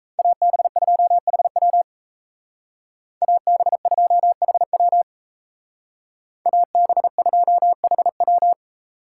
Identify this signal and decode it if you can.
Morse code, heard as a keyed tone, about 36 wpm, 700 Hz. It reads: AB2HW AB2HW AB2HW